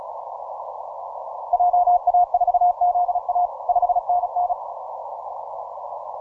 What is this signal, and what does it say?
Morse code, heard as a keyed tone, about 36 wpm, 710 Hz. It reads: JA4ZA 5NN